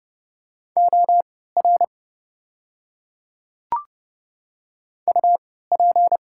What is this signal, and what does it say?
Morse code, heard as a keyed tone, about 31 wpm, spaced as for 17 wpm, 700 Hz. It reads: OR UP
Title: 3x Character Spacing - Top 100 Words - 30wpm